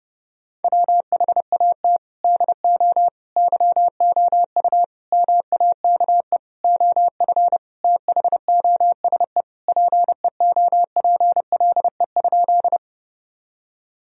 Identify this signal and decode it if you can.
Morse code, heard as a keyed tone, about 30 wpm, 700 Hz. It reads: WHAT DO YOU MAKE OF THOSE PEOPLE?
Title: Sentences from Top 1000 Words - Review Part 1 - 30wpm